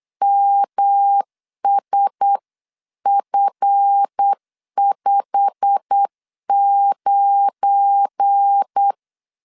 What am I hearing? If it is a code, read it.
Morse code, heard as a keyed tone, about 8 wpm, 800 Hz. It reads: MSF59
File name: 8wpm